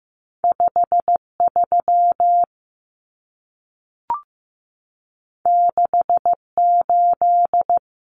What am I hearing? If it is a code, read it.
Morse code, heard as a keyed tone, about 15 wpm, 700 Hz. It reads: 53 68
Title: Common Cut Numbers - 2-Digit 15wpm